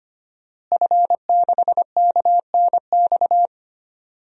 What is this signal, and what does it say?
Morse code, heard as a keyed tone, about 25 wpm, 700 Hz. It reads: F6KNX
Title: DXCC-100 Call Signs - 5 Characters - Repeated 3x - 25wpm